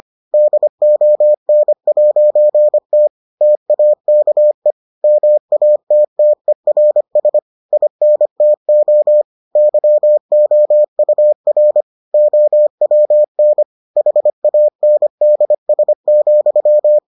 Morse code, heard as a keyed tone, about 25 wpm, 600 Hz. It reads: DON'T TAKE MATTERS INTO YOUR OWN HANDS,